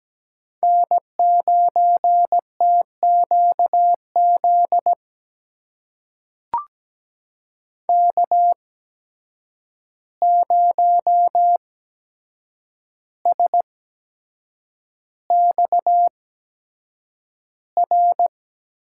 Morse code, heard as a keyed tone, about 17 wpm, spaced as for 7 wpm, 700 Hz. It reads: N9TQZ K0SXR
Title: USA Call Signs - 5 Characters - 8x Character Spacing - 17wpm